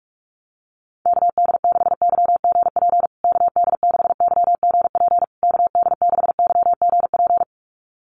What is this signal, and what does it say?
Morse code, heard as a keyed tone, about 44 wpm, 700 Hz. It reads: KD6YGP KD6YGP KD6YGP
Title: USA Call Signs Repeated 3x 45wpm